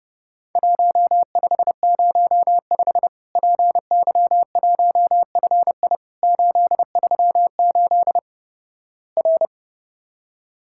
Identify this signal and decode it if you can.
Morse code, heard as a keyed tone, about 30 wpm, 700 Hz. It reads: 1505 PY1FI 838 R
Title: Worked All Europe DX Contest - QTC Exchange - 30wpm